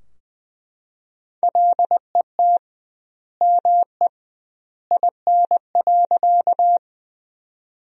Morse code, heard as a keyed tone, about 21 wpm, 700 Hz. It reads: LET ME IN.
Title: 2x Word-Spacing - Sentences from Top 300 Words - 20wpm